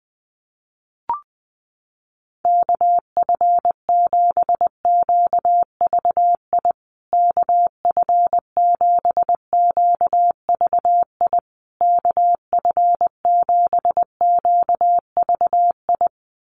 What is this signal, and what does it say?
Morse code, heard as a keyed tone, about 20 wpm, 700 Hz. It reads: KF7QVI KF7QVI KF7QVI
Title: USA Call Signs Repeated 3x 20wpm